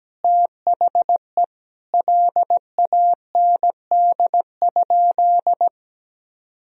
Morse code, heard as a keyed tone, about 17 wpm, 700 Hz. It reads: THE LAND?
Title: Sentences from Top 1000 Words - Review Part 3 - 17wpm